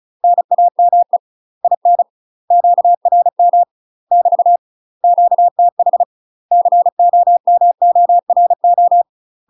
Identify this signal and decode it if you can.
Morse code, heard as a keyed tone, about 36 wpm, 700 Hz. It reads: NAME IN QRM <BT> QTH COMORO
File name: UK MASTER QSOs TRAINING / 35WPM / UK MASTER QSOs 0020